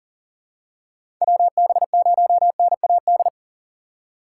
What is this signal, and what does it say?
Morse code, heard as a keyed tone, about 40 wpm, 700 Hz. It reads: WB0NAD
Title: USA Call Signs - 6 Characters - 4x Character Spacing - 40wpm